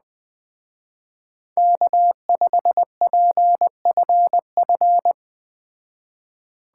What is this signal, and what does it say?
Morse code, heard as a keyed tone, about 20 wpm, 700 Hz. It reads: K5PFF
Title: Speed Racing - USA Call Signs 17wpm